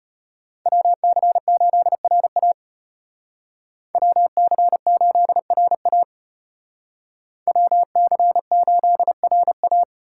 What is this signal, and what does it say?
Morse code, heard as a keyed tone, about 33 wpm, 700 Hz. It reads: WC8RA WC8RA WC8RA